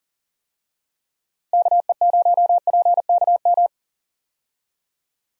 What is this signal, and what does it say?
Morse code, heard as a keyed tone, about 40 wpm, 700 Hz. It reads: KE0PKM
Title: USA Call Signs - 6 Characters - Repeated 3x - 40wpm